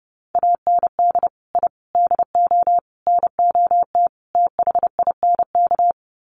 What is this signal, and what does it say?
Morse code, heard as a keyed tone, about 30 wpm, 700 Hz. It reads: AND I DO NOT THINK
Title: Sets of 5 Words 30wpm